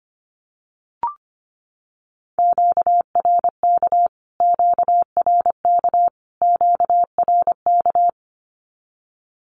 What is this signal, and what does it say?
Morse code, heard as a keyed tone, about 25 wpm, 700 Hz. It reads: QRK QRK QRK
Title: Q-Codes Repeated 3x 25wpm